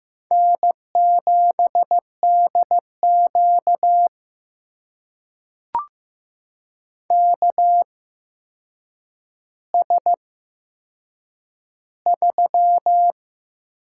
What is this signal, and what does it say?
Morse code, heard as a keyed tone, about 15 wpm, spaced as for 7 wpm, 700 Hz. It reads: N7DQ KS3
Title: USA Call Signs - 4 Characters - 8x Character Spacing - 15wpm